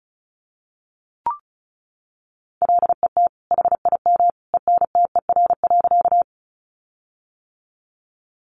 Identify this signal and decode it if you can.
Morse code, heard as a keyed tone, about 34 wpm, 700 Hz. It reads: LET HIM ENTER.
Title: Sentences from Top 1300 Words 35wpm